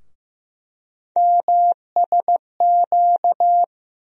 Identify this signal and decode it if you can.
Morse code, heard as a keyed tone, about 15 wpm, 700 Hz. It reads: MSQ
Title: USA Call Sign Postfixes - 3 Letters 15wpm